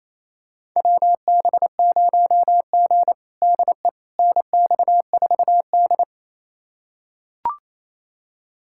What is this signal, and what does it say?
Morse code, heard as a keyed tone, about 28 wpm, 700 Hz. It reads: WB0G DE NX4D
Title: USA Callsign DE Callsign - 4 Characters - 28wpm